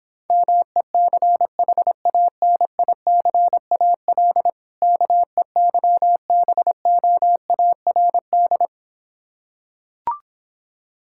Morse code, heard as a keyed tone, about 26 wpm, 700 Hz. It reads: MECHANICAL KEYBOARD